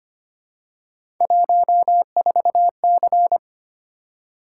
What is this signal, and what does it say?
Morse code, heard as a keyed tone, about 25 wpm, 700 Hz. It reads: 14C
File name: Three Letter-Number Combinations - 25wpm